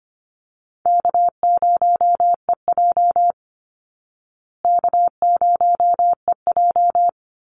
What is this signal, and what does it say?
Morse code, heard as a keyed tone, about 25 wpm, 700 Hz. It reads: K0EJ K0EJ
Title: Easy Call Signs Repeated 2x - 25wpm